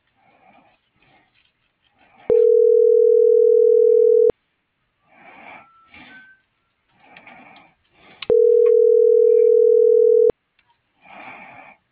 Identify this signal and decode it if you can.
Ringback tone (North America), 440 + 480 Hz.